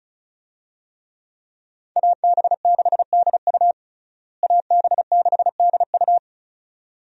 Morse code, heard as a keyed tone, about 36 wpm, 700 Hz. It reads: AB6DU AB6DU
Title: USA Call Signs - 5 Characters - Repeated 2x - 35wpm